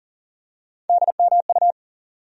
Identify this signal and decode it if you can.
Morse code, heard as a keyed tone, about 40 wpm, 700 Hz. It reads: DMU
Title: USA Call Sign Postfixes - 3 Letters 40wpm